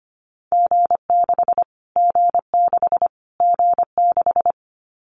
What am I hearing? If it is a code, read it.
Morse code, heard as a keyed tone, about 25 wpm, 700 Hz. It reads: G6 G6 G6
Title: Callsign Prefixes for Top 25 DXCC - Repeated 3x - 25wpm